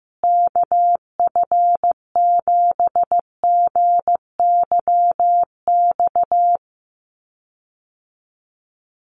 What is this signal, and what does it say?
Morse code, heard as a keyed tone, about 15 wpm, 700 Hz. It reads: KF7GYX
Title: USA Call Signs Repeated 3x 15wpm